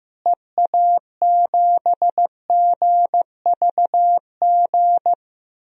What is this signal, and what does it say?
Morse code, heard as a keyed tone, about 15 wpm, 700 Hz. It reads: EA7GVG
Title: DXCC-100 Call Signs - 6 Characters - Repeated 3x - 15wpm